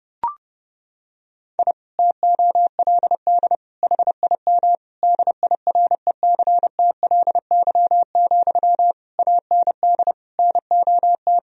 Morse code, heard as a keyed tone, about 32 wpm, 700 Hz. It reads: I TOLD HIM DIRECTLY, AND NOT
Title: Sentences from Top 1000 Words - Review Part 1 - 30wpm